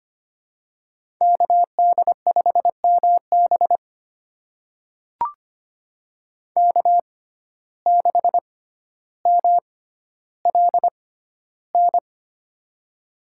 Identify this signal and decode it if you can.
Morse code, heard as a keyed tone, about 25 wpm, spaced as for 13 wpm, 700 Hz. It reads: KD5MB K6MLN
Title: USA Call Signs - 5 Characters - 6x Character Spacing - 25wpm